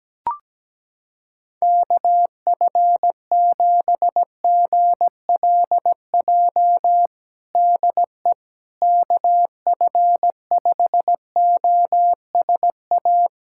Morse code, heard as a keyed tone, about 17 wpm, 700 Hz. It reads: KF7GLJ DE KF5OSA
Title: USA Callsign DE Callsign - 6 Characters - 17wpm